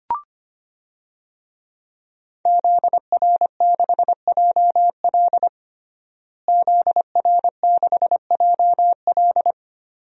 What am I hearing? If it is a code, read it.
Morse code, heard as a keyed tone, about 25 wpm, 700 Hz. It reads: ZR6JL ZR6JL
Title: DXCC-100 Call Signs - 5 Characters - Repeated 2x - 25wpm